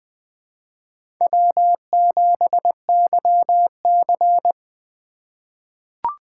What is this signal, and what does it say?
Morse code, heard as a keyed tone, about 20 wpm, 700 Hz. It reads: W7YC
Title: USA Call Signs - 4 Characters - 3x Character Spacing - 20wpm